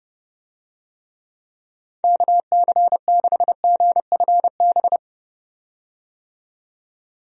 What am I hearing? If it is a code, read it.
Morse code, heard as a keyed tone, about 29 wpm, 700 Hz. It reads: KC6GFB
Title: USA Call Signs - 6 Characters - 30wpm